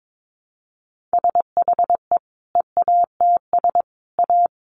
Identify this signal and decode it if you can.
Morse code, heard as a keyed tone, about 22 wpm, 700 Hz. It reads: SHE EATS A
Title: Additional Easy Phrases 22wpm